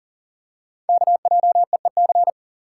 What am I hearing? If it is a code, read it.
Morse code, heard as a keyed tone, about 42 wpm, 700 Hz. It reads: KJEEC
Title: Code Groups - 5 Letters - 40wpm